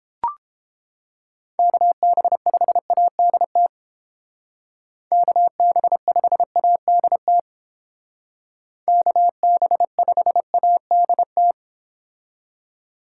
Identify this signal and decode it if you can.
Morse code, heard as a keyed tone, about 29 wpm, 700 Hz. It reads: KB5ADT KB5ADT KB5ADT